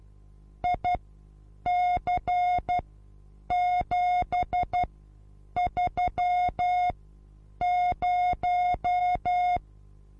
Morse code, heard as a keyed tone, about 12 wpm, spaced as for 8 wpm, 720 Hz. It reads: IC730
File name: Onlinelehrgang / morsen / DJ4UF Morsen23